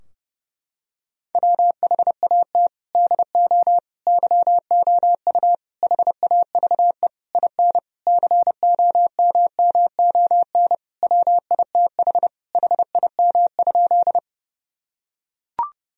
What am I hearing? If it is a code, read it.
Morse code, heard as a keyed tone, about 30 wpm, 700 Hz. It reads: WHAT DO YOU HAVE IN COMMON WITH HIM?
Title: Sentences from Top 1000 Words - Review Part 3 - 30wpm